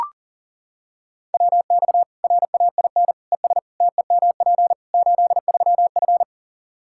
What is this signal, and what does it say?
Morse code, heard as a keyed tone, about 40 wpm, 700 Hz. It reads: WX RAIN ES TEMP 83F
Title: Practice QSO - Exchange 2 - Info - 40wpm